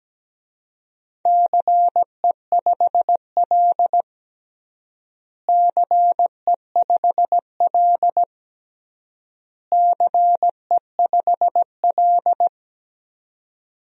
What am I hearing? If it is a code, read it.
Morse code, heard as a keyed tone, about 17 wpm, 700 Hz. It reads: CE5L CE5L CE5L
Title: DXCC-100 Call Signs - 4 Characters - Repeated 3x - 17wpm